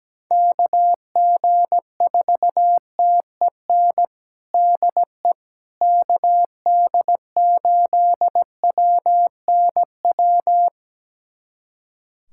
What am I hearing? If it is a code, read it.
Morse code, heard as a keyed tone, about 17 wpm, 700 Hz. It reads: KG4TEN DE KD8WNW
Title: USA Callsign DE Callsign - 6 Characters - 17wpm